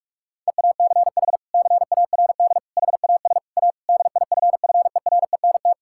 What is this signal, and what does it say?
Morse code, heard as a keyed tone, about 46 wpm, 700 Hz. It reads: EACH CARD HAS A DIFFERENT
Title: Sentences from Top 1000 Words - Review Part 6 - 45wpm